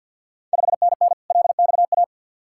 Morse code, heard as a keyed tone, about 52 wpm, 700 Hz. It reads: 5NN LXA
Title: Ontario QSO Party Exchange 50wpm